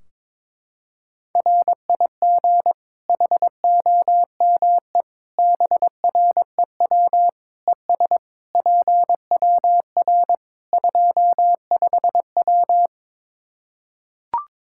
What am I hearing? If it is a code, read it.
Morse code, heard as a keyed tone, about 22 wpm, 700 Hz. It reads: RIG HOME BREW ES PWR 25W
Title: Practice QSO - Exchange 2 - Info - 22wpm